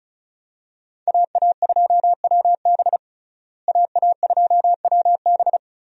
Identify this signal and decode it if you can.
Morse code, heard as a keyed tone, about 36 wpm, 700 Hz. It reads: AA2WB AA2WB
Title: USA Call Signs - 5 Characters - Repeated 3x - 35wpm